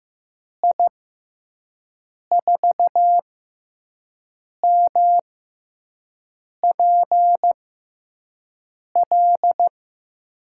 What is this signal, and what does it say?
Morse code, heard as a keyed tone, about 15 wpm, spaced as for 5 wpm, 700 Hz. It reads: I4MPL
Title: USA Call Signs - 6x Character Spacing - 15wpm